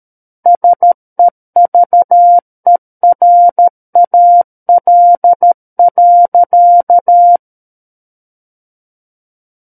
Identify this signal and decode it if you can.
Morse code, heard as a keyed tone, about 13 wpm, 700 Hz. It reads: SEVERAL.
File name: Book One / 04-Chapter 4 - 13 WPM